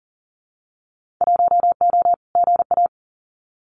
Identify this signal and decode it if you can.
Morse code, heard as a keyed tone, about 39 wpm, 700 Hz. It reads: 1O GA
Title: Winter Field Day Exchanges 40wpm